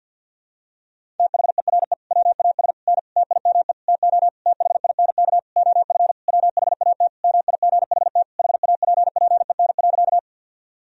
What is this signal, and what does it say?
Morse code, heard as a keyed tone, about 49 wpm, 700 Hz. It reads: THERE WAS N TIME TO THINK OF WHAT MIGHT HAPPEN.
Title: Sentences from Top 400 Words and Abbreviations 50wpm